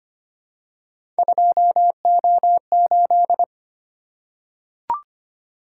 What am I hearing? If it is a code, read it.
Morse code, heard as a keyed tone, about 25 wpm, 700 Hz. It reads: 2O8
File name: Three Letter-Number Combinations - 25wpm